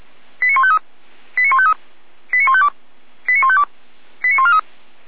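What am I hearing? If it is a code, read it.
Five-tone selective call (ZVEI), heard as quick runs of five tones, tones of about 70 ms. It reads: 89153 89153 89152 89152 89143
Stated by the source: ZVEI five-tone selective call 89126 received by SDR.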